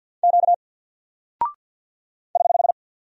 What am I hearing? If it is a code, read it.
Morse code, heard as a keyed tone, about 48 wpm, 700 Hz. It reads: Q <HH>
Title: Single Character (Letter:Number:Punctuation:Prosign) Alt 50wpm